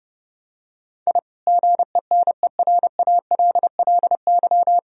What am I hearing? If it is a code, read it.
Morse code, heard as a keyed tone, about 30 wpm, 700 Hz. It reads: I GENERALLY
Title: Sentences from Top 1000 Words - Review Part 4 - 30wpm